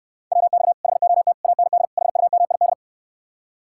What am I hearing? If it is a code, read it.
Morse code, heard as a keyed tone, about 68 wpm, 700 Hz. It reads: WX HOT AND HUMID